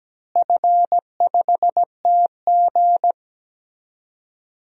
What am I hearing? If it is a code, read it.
Morse code, heard as a keyed tone, about 17 wpm, 700 Hz. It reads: F5TG
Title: DXCC-100 Call Signs - 4 Characters - Repeated 2x - 17wpm